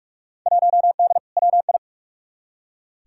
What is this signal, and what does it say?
Morse code, heard as a keyed tone, about 46 wpm, 700 Hz. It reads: 1D WI
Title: Field Day Exchanges - 45wpm